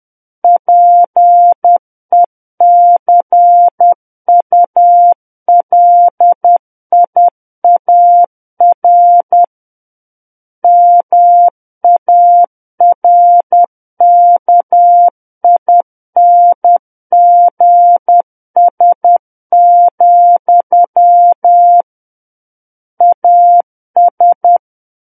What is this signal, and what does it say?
Morse code, heard as a keyed tone, about 10 wpm, 700 Hz. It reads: PECULIAR MARKINGS, AS
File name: Book One / 01-Chapter 1 - 10 WPM